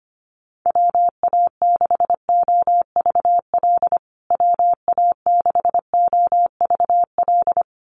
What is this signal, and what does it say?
Morse code, heard as a keyed tone, about 25 wpm, 700 Hz. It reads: WA6OVL WA6OVL
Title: USA Call Signs Repeated 2x 25wpm